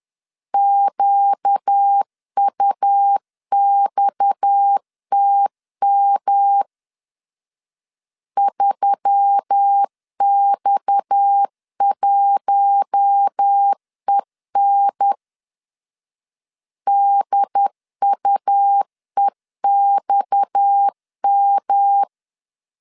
Morse code, heard as a keyed tone, about 10 wpm, 800 Hz. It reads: QUXTM 3X1EN DUEXM
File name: LEZIONI / 6 PKX0 / 6 PKX 10wpm 2426